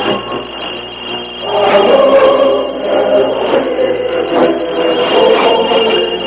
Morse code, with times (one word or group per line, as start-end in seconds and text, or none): none